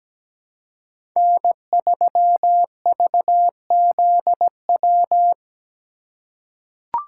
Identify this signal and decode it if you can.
Morse code, heard as a keyed tone, about 17 wpm, 700 Hz. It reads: N3VZW
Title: USA Call Signs - 5 Characters - 4x Character Spacing - 17wpm